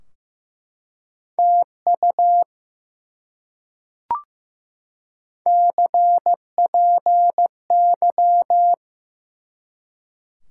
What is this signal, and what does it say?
Morse code, heard as a keyed tone, about 15 wpm, 700 Hz. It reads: TU CPY
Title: Top 90 QSO Elements 15wpm